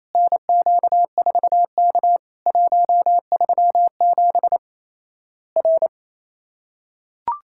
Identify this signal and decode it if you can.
Morse code, heard as a keyed tone, about 28 wpm, 700 Hz. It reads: NQ4K 137 R
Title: Worked All Europe DX Contest - QTC Exchange - 28wpm